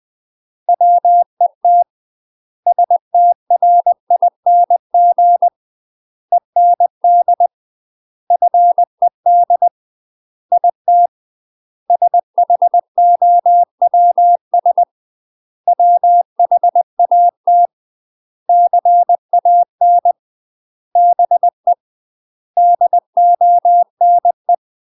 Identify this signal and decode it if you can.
Morse code, heard as a keyed tone, about 20 wpm, 700 Hz. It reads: WET STRING END FED IT SHOWS WHAT CAN BE DONE
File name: UK MASTER QSOs TRAINING/20WPM